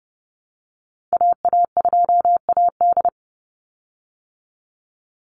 Morse code, heard as a keyed tone, about 30 wpm, 700 Hz. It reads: AA2AD
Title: Call Signs Repeated 2x - 30wpm